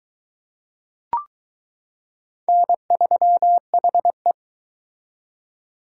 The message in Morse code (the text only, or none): N3HE